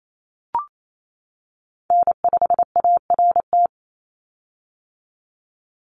Morse code, heard as a keyed tone, about 28 wpm, 700 Hz. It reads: N5ART
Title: USA Call Signs 28wpm